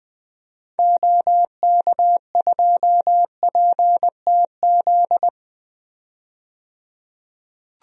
Morse code, heard as a keyed tone, about 20 wpm, 700 Hz. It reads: OK2PTZ
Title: DXCC-100 Call Signs - 6 Characters - 20wpm